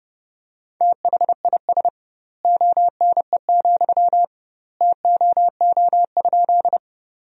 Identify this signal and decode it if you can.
Morse code, heard as a keyed tone, about 29 wpm, 700 Hz. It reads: THIS ONE, TOO?
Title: 2x Word-Spacing - Sentences from Top 200 Words - 30wpm